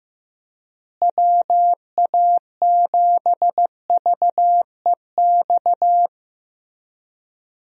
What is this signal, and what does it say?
Morse code, heard as a keyed tone, about 15 wpm, 700 Hz. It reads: WA7VEX
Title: USA Call Signs - 6 Characters - Repeated 2x - 15wpm